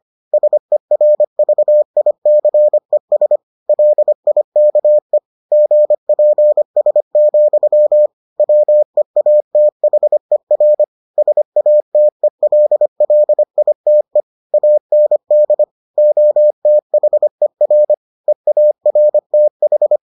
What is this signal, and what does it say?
Morse code, heard as a keyed tone, about 25 wpm, 600 Hz. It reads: SERVICES LIKE GPS, WEATHER SATELLITE AND OTHER EARTH